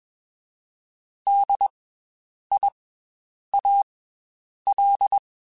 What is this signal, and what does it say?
Morse code, heard as a keyed tone, about 21 wpm, 800 Hz. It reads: D I A L
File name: Special INGs / Special INGs-mp3s / Slow ing @20